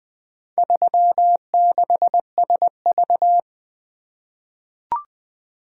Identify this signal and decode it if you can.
Morse code, heard as a keyed tone, about 21 wpm, 700 Hz. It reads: 36SV
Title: Four Letter-Number Combinations - 20wpm